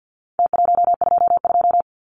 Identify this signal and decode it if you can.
Morse code, heard as a keyed tone, about 49 wpm, 700 Hz. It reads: T122
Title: Numbers - Contest Style - 4-Digit 50wpm